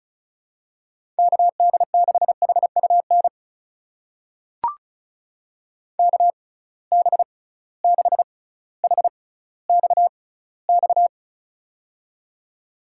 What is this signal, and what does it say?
Morse code, heard as a keyed tone, about 36 wpm, spaced as for 18 wpm, 700 Hz. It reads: KD6HUN KB6HXX
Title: USA Call Signs - 6 Characters - 6x Character Spacing - 35wpm